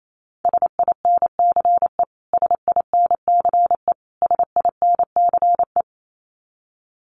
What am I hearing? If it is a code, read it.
Morse code, heard as a keyed tone, about 28 wpm, 700 Hz. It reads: SINCE SINCE SINCE
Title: Top 500 Words Repeated 28wpm